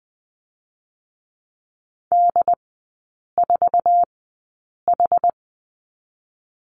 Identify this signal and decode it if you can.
Morse code, heard as a keyed tone, about 21 wpm, spaced as for 8 wpm, 700 Hz. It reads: D4H
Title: 107 Lesson 18e - Letter m - Three Characters 20wpm